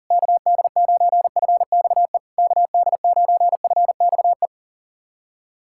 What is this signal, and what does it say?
Morse code, heard as a keyed tone, about 40 wpm, 690 Hz. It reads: KD9FXE KD9FXE
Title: SOTA QSO Practice 40wpm